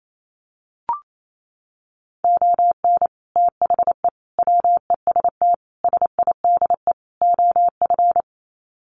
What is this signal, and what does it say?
Morse code, heard as a keyed tone, about 28 wpm, 700 Hz. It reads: ON THE WEST SIDE OF
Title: Sets of 5 Words Encore2 28wpm